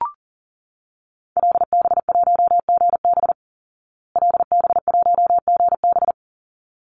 Morse code, heard as a keyed tone, about 40 wpm, 700 Hz. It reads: LB1GB LB1GB
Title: Call Signs Repeated 3x - 40wpm